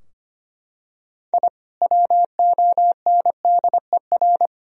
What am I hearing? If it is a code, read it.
Morse code, heard as a keyed tone, about 25 wpm, 700 Hz. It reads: I WONDER
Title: Sentences from Top 1000 Words - Review Part 3 - 25wpm